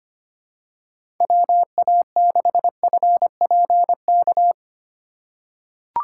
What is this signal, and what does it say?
Morse code, heard as a keyed tone, about 25 wpm, 700 Hz. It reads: WA6FPK
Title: USA Call Signs - 6 Characters - 4x Character Spacing - 25wpm